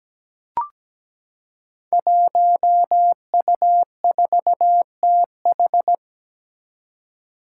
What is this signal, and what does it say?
Morse code, heard as a keyed tone, about 17 wpm, 700 Hz. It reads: 1U4TH
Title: Five Challenging Letter-Number Combinations 17wpm